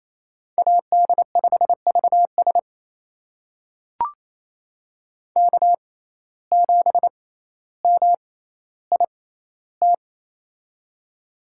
Morse code, heard as a keyed tone, about 28 wpm, spaced as for 14 wpm, 700 Hz. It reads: AD5VS K7MIT
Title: USA Call Signs - 5 Characters - 6x Character Spacing - 28wpm